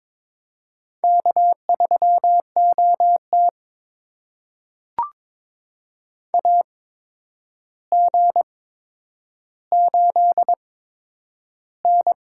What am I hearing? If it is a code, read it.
Morse code, heard as a keyed tone, about 22 wpm, spaced as for 9 wpm, 700 Hz. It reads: K3OT AG8N
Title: USA Call Signs - 4 Characters - 8x Character Spacing - 22wpm